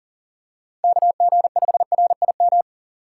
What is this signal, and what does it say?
Morse code, heard as a keyed tone, about 40 wpm, 700 Hz. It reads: KG5RIM
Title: USA Call Signs - 6 Characters - 3x Character Spacing - 40wpm